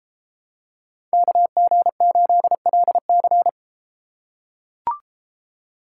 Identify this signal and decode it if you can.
Morse code, heard as a keyed tone, about 33 wpm, 700 Hz. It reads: KG8LC